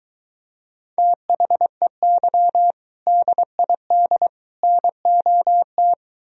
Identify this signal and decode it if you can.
Morse code, heard as a keyed tone, about 24 wpm, 700 Hz. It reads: THEY DID NOT